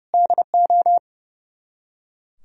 Morse code, heard as a keyed tone, about 38 wpm, 700 Hz. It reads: DO